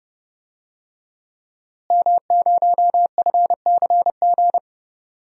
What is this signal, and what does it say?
Morse code, heard as a keyed tone, about 30 wpm, 700 Hz. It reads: M0FCG
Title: DXCC-100 Call Signs - 5 Characters - Repeated 2x - 30wpm